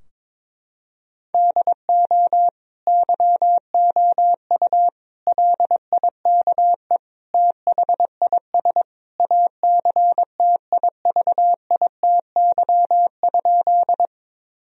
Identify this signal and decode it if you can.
Morse code, heard as a keyed tone, about 22 wpm, 700 Hz. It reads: DO YOU LIKE THIS ACTIVITY?
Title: Sentences from Top 1000 Words - Review Part 4 - 22wpm